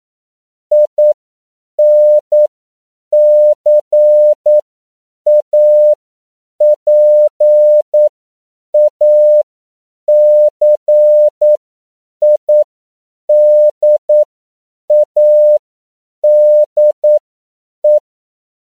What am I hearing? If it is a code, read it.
Morse code, heard as a keyed tone, about 9 wpm, spaced as for 7 wpm, 600 Hz. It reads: INCAPACIDADE